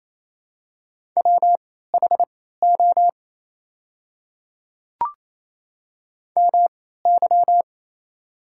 Morse code, heard as a keyed tone, about 28 wpm, spaced as for 16 wpm, 700 Hz. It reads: WHO MY
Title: 3x Character Spacing - Top 100 Words - 28wpm